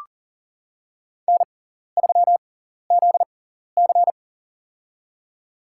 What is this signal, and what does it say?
Morse code, heard as a keyed tone, about 37 wpm, spaced as for 14 wpm, 700 Hz. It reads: N3ZC
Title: USA Call Signs - 4 Characters - 6x Character Spacing - 40wpm